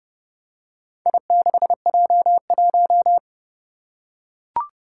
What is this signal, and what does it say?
Morse code, heard as a keyed tone, about 30 wpm, 700 Hz. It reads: I6J1
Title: Four Challenging Letter-Number Combinations 30wpm